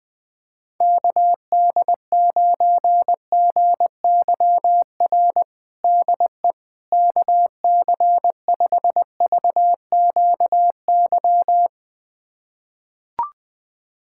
Morse code, heard as a keyed tone, about 20 wpm, 700 Hz. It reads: KD9GYR DE KC5VQY
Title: USA Callsign DE Callsign - 6 Characters - 20wpm